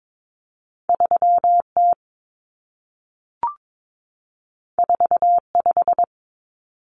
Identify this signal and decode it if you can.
Morse code, heard as a keyed tone, about 22 wpm, 700 Hz. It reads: 3T 45
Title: Common Cut Numbers - 2-Digit 22wpm